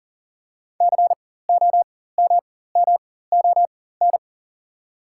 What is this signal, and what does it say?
Morse code, heard as a keyed tone, about 37 wpm, spaced as for 18 wpm, 690 Hz. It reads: COMMON
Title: 4x Character Spacing - Top 500 Words - 40wpm